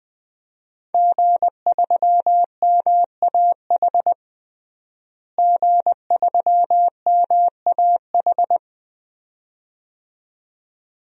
Morse code, heard as a keyed tone, about 20 wpm, 700 Hz. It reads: G3MAH G3MAH
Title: DXCC-100 Call Signs - 5 Characters - Repeated 2x - 20wpm